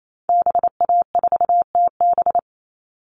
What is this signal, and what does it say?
Morse code, heard as a keyed tone, about 28 wpm, 700 Hz. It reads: BA4TB
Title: Call Signs Repeated 3x - 28wpm